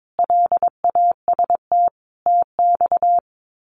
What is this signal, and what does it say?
Morse code, heard as a keyed tone, about 22 wpm, 700 Hz. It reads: LAST TX
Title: Common QSO Phrases 22wpm